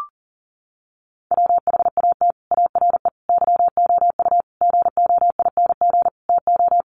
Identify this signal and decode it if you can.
Morse code, heard as a keyed tone, about 40 wpm, 700 Hz. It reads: WHAT ARE YOU GOING TO
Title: Sets of 5 Words 40wpm